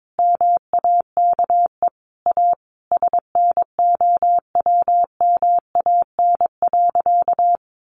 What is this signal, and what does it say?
Morse code, heard as a keyed tone, about 22 wpm, 700 Hz. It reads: MAKE A SNOWMAN.
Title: Easy Phrases 22wpm